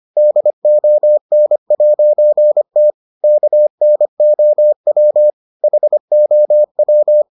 Morse code, heard as a keyed tone, about 25 wpm, 600 Hz. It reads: DON'T KNOW HOW